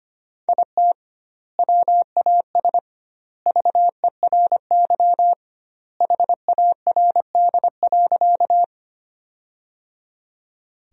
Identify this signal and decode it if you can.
Morse code, heard as a keyed tone, about 25 wpm, 700 Hz. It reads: IT WAS VERY HARD.
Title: 2x Word-Spacing - Sentences from Top 300 Words - 25wpm